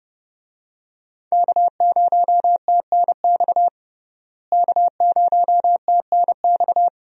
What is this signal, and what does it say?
Morse code, heard as a keyed tone, about 32 wpm, 700 Hz. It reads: K0TNX K0TNX
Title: USA Call Signs - 5 Characters - Repeated 3x - 30wpm